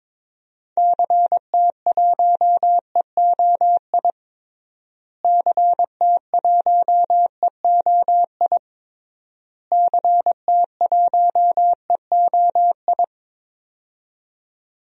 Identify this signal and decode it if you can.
Morse code, heard as a keyed tone, about 22 wpm, 700 Hz. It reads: CT1EOI CT1EOI CT1EOI
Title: DXCC-100 Call Signs - 6 Characters - Repeated 3x - 22wpm